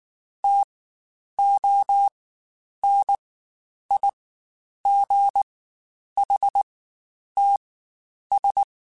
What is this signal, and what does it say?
Morse code, heard as a keyed tone, about 19 wpm, spaced as for 9 wpm, 800 Hz. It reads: TONIGHTS
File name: Spaced Characters / Spaced Characters-mp3s / Weather 4 14 sp @18